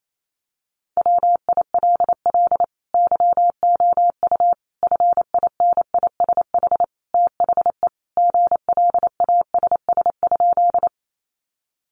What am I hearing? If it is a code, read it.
Morse code, heard as a keyed tone, about 28 wpm, 700 Hz. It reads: WILL YOU FINISH THE GLASS?
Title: Sentences from Top 1300 Words 28wpm